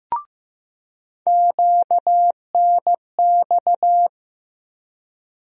Morse code, heard as a keyed tone, about 15 wpm, 700 Hz. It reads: QNX
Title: USA Call Sign Postfixes - 3 Letters 15wpm